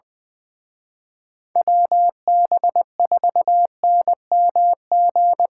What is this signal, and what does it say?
Morse code, heard as a keyed tone, about 23 wpm, 700 Hz. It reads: WB4NMG
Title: Speed Racing - USA 2x3 Call Signs 20wpm